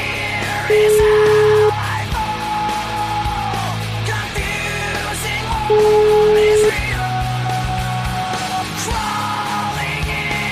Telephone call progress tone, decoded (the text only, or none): ringback tone